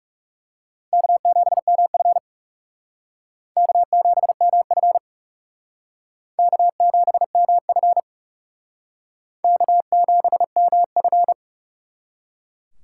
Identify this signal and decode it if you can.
Morse code, heard as a keyed tone, about 39 wpm, 700 Hz. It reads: K7MF K7MF K7MF K7MF